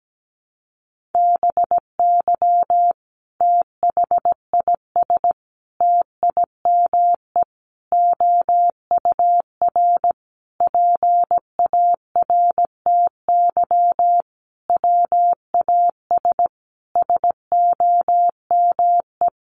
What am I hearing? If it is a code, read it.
Morse code, heard as a keyed tone, about 17 wpm, 700 Hz. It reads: BY THIS TIME OUR PARTY WAS SOME
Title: Sentences from Top 1400 Words 17wpm